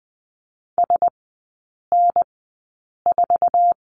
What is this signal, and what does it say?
Morse code, heard as a keyed tone, about 20 wpm, spaced as for 8 wpm, 700 Hz. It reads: SN4
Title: 074 Lesson 13e - Letter L - Three Characters 20wpm